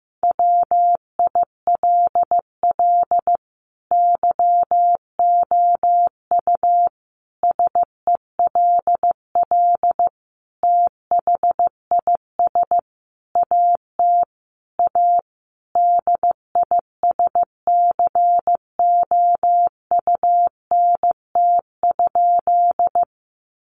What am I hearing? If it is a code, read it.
Morse code, heard as a keyed tone, about 15 wpm, 700 Hz. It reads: WILL YOU SELL THIS AT A DISCOUNT?
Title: Sentences from Top 1200 Words 15wpm